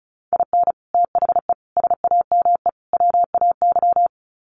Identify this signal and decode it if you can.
Morse code, heard as a keyed tone, about 34 wpm, 700 Hz. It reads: IN THE SAME WAY
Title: Sets of 4 Words 35wpm